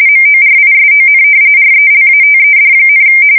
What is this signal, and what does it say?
RTTY, about 45.45 baud. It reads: MORISON RELATED,
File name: fsk170 / battleshipnj